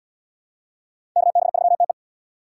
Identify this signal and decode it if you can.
Morse code, heard as a keyed tone, about 71 wpm, 700 Hz. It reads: KB3TE